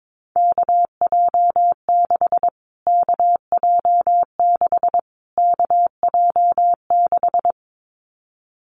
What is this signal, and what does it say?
Morse code, heard as a keyed tone, about 22 wpm, 700 Hz. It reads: KJ6 KJ6 KJ6
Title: Callsign Prefixes for Top 25 DXCC - Repeated 3x - 22wpm